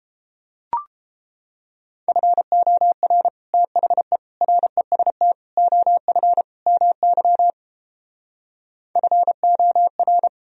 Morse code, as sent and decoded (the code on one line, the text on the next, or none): ..-. --- .-. / - .... / .-. ... - / --- ..-. / -- -.-- / ..-. --- .-.
FOR THE REST OF MY FOR